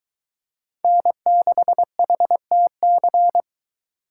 Morse code, heard as a keyed tone, about 22 wpm, 700 Hz. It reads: N6HTC